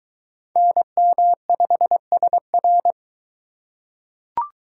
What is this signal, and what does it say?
Morse code, heard as a keyed tone, about 21 wpm, 700 Hz. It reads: NM5SR